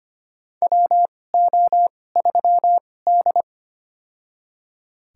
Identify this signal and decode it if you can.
Morse code, heard as a keyed tone, about 25 wpm, spaced as for 18 wpm, 700 Hz. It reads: WO3D
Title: USA Call Signs - 4 Characters - 2x Character Spacing - 25wpm